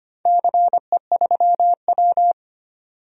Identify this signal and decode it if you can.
Morse code, heard as a keyed tone, about 25 wpm, 700 Hz. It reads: CE3W
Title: DXCC-100 Call Signs - 4 Characters - 25wpm